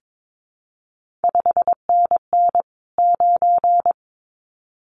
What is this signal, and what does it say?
Morse code, heard as a keyed tone, about 22 wpm, 700 Hz. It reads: 5NN 9
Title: CQ WW DX Contest Exchange Encore - 22wpm